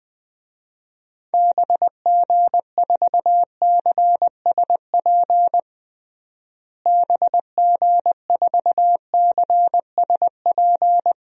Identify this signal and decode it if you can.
Morse code, heard as a keyed tone, about 21 wpm, 700 Hz. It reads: BG4CSP BG4CSP
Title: DXCC-100 Call Signs - 6 Characters - Repeated 3x - 20wpm